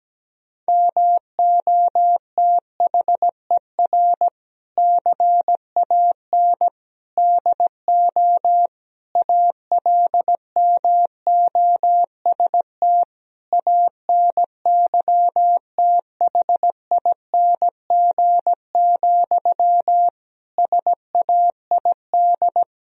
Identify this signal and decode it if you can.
Morse code, heard as a keyed tone, about 17 wpm, 700 Hz. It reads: MOTHER CAN DO ALMOST ANYTHING, SAID
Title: Sentences from Top 1000 Words - Review Part 4 - 17wpm